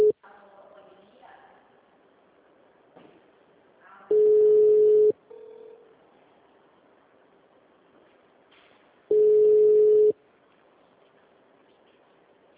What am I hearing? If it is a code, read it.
Ringback tone (Europe, 425 Hz), with speech.
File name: out- -2022- - - .87